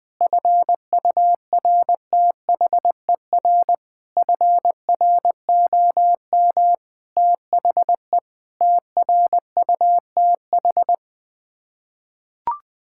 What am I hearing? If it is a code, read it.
Morse code, heard as a keyed tone, about 20 wpm, 700 Hz. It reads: FURTHER FROM THE TRUTH
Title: Sets of 8 Words - 20wpm